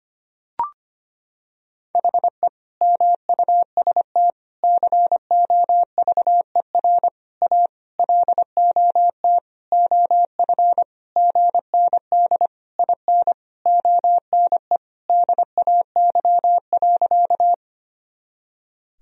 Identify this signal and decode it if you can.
Morse code, heard as a keyed tone, about 25 wpm, 700 Hz. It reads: HE MUST COVER A LOT OF GND IN ONE DAY.
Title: Sentences from Top 500 Words and Abbreviations 25wpm